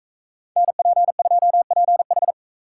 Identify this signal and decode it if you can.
Morse code, heard as a keyed tone, about 44 wpm, 700 Hz. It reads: NP2PH